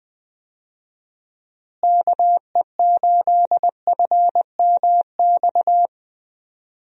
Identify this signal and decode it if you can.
Morse code, heard as a keyed tone, about 20 wpm, 700 Hz. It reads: KE8FMX
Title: USA Call Signs - 6 Characters - Repeated 3x - 20wpm